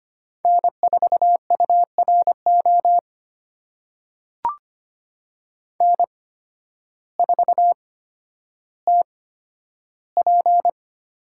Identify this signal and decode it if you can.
Morse code, heard as a keyed tone, about 25 wpm, spaced as for 12 wpm, 700 Hz. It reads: N4URO N4TP
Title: USA Call Signs - 5 Characters - 8x Character Spacing - 25wpm